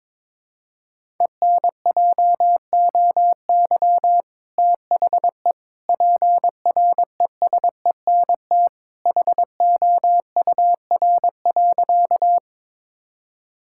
Morse code, heard as a keyed tone, about 23 wpm, 700 Hz. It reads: ENJOY THE PRESENT HOUR.
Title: Sentences from Top 1000 Words - Review Part 5 - 22wpm